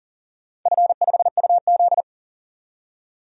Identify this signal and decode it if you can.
Morse code, heard as a keyed tone, about 38 wpm, 700 Hz. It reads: F5UZ
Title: DXCC-100 Call Signs - 4 Characters - 40wpm